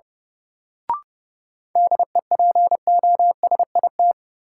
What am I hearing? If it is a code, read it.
Morse code, heard as a keyed tone, about 30 wpm, 700 Hz. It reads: DEPOSIT
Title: Sentences from Top 1300 Words - New Word Drill - 30wpm